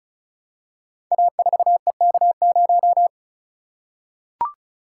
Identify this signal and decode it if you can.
Morse code, heard as a keyed tone, about 36 wpm, 700 Hz. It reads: A4EK0
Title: Code Groups - 5 Letter:Numbers - 35wpm